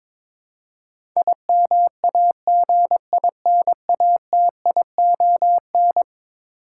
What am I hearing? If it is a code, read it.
Morse code, heard as a keyed tone, about 22 wpm, 700 Hz. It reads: IMAGINATION
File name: practice / 22wpm / Eleven Letter Words - 22wpm